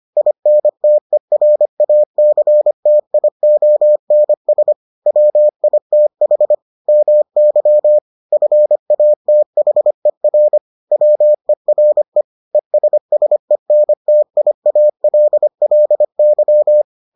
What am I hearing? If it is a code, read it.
Morse code, heard as a keyed tone, about 25 wpm, 600 Hz. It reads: INTERACTIONS WITH MY FATHER WERE ESSENTIALLY